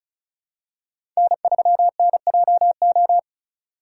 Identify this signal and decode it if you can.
Morse code, heard as a keyed tone, about 34 wpm, 690 Hz. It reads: N3NJO
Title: USA Call Signs - 5 Characters - 3x Character Spacing - 35wpm